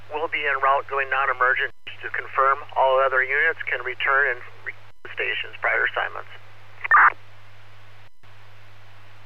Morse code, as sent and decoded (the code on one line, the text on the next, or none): none